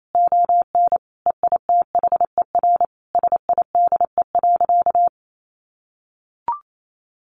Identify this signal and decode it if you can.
Morse code, heard as a keyed tone, about 28 wpm, 700 Hz. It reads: ON EITHER SIDE.
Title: Sentences from Top 1200 Words 28wpm